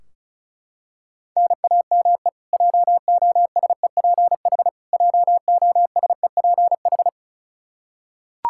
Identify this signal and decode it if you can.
Morse code, heard as a keyed tone, about 34 wpm, 700 Hz. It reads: NAME JOSEPH JOSEPH
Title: K1USN SST QSO Practice 35wpm